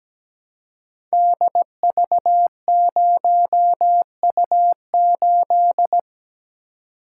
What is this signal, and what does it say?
Morse code, heard as a keyed tone, about 17 wpm, 690 Hz. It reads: DV0U8
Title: Code Groups - 5 Letter:Numbers - 17wpm